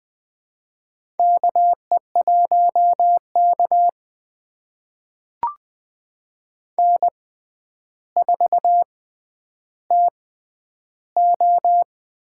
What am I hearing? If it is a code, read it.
Morse code, heard as a keyed tone, about 20 wpm, spaced as for 10 wpm, 700 Hz. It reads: KE1K N4TO
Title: USA Call Signs - 4 Characters - 6x Character Spacing - 20wpm